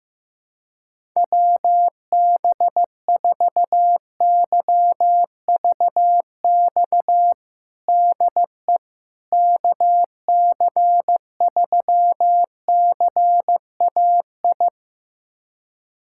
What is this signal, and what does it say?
Morse code, heard as a keyed tone, about 15 wpm, 700 Hz. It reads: WB4YVX DE KC3CAI
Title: USA Callsign DE Callsign - 6 Characters - 15wpm